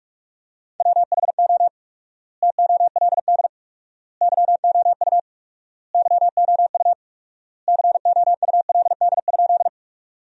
Morse code, heard as a keyed tone, about 46 wpm, 700 Hz. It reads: WHO TOLD YOU YOU COULD?
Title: 4x Word-Spacing - Sentences from Top 400 Words - 45wpm